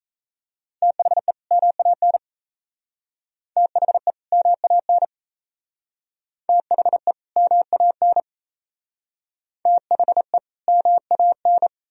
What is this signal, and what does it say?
Morse code, heard as a keyed tone, about 34 wpm, 700 Hz. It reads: THE MAN THE MAN THE MAN THE MAN
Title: Speed-Racing - Sets of 2 Words 28wpm